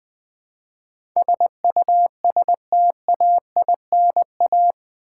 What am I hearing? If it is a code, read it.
Morse code, heard as a keyed tone, about 20 wpm, 700 Hz. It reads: SUSTAINA
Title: Build A Word - 4 Letters At A Time - 11-Letter Words - 20wpm